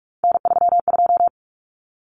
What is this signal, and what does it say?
Morse code, heard as a keyed tone, about 44 wpm, 700 Hz. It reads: N32
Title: Common Cut Numbers - 3-Digit 45wpm